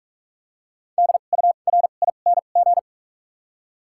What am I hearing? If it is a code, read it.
Morse code, heard as a keyed tone, about 46 wpm, spaced as for 32 wpm, 700 Hz. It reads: DURING
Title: 2x Character Spacing - Top 500 Words - 45wpm